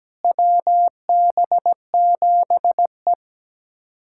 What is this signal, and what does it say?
Morse code, heard as a keyed tone, about 17 wpm, 700 Hz. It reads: WB7E
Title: USA Callsign DE Callsign - 4 Characters - 17wpm